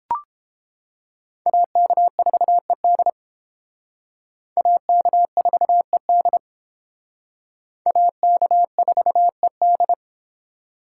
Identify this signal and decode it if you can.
Morse code, heard as a keyed tone, about 30 wpm, 700 Hz. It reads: AK4ED AK4ED AK4ED